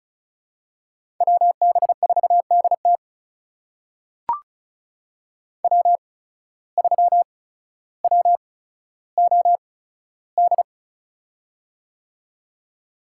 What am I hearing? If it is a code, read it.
Morse code, heard as a keyed tone, about 36 wpm, spaced as for 15 wpm, 700 Hz. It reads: WB4DT W3WOD
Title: USA Call Signs - 5 Characters - 8x Character Spacing - 35wpm